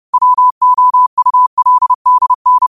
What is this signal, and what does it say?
Morse code, heard as a keyed tone, about 30 wpm, 1000 Hz. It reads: WOULDN
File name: qotdfiles / 30wpm